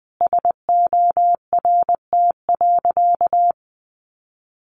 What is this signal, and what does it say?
Morse code, heard as a keyed tone, about 20 wpm, 700 Hz. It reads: SORT.
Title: Sentences from Top 1300 Words 20wpm